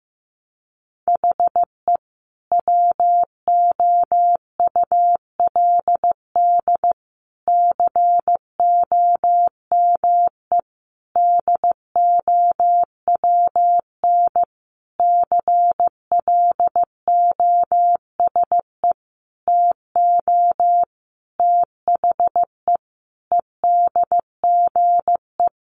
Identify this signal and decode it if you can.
Morse code, heard as a keyed tone, about 15 wpm, 700 Hz. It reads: HE WOULD COME DOWN CLOSE TO THE EDGE
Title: Sentences from Top 1300 Words 15wpm